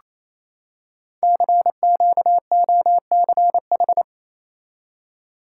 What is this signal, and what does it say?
Morse code, heard as a keyed tone, about 27 wpm, 700 Hz. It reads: CQOCH
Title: Code Groups - 5 Letter:Numbers - 28wpm